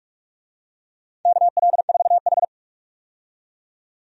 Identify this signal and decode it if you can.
Morse code, heard as a keyed tone, about 42 wpm, 700 Hz. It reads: KL4H
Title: USA Call Signs - 4 Characters - 45wpm